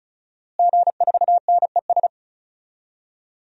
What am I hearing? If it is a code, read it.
Morse code, heard as a keyed tone, about 36 wpm, 700 Hz. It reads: G4NES
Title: DXCC-100 Call Signs - 5 Characters - Repeated 2x - 35wpm